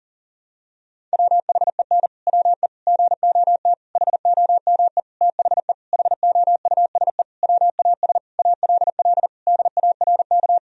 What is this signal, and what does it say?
Morse code, heard as a keyed tone, about 40 wpm, 700 Hz. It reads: WHEN WE GOT HOME THE HOUSE WAS ALL DARK
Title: Sentences from Top 500 Words and Abbreviations 40wpm